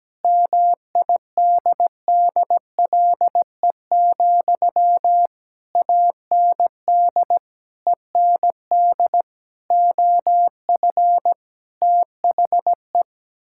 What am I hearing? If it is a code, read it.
Morse code, heard as a keyed tone, about 17 wpm, 700 Hz. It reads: MIDDLE, AND END OF THE
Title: Sentences from Top 1000 Words - Review Part 6 - 17wpm